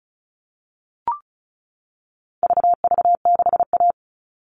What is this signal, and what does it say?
Morse code, heard as a keyed tone, about 34 wpm, 700 Hz. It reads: VV6A
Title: All Cut Numbers - 4-Digit 35wpm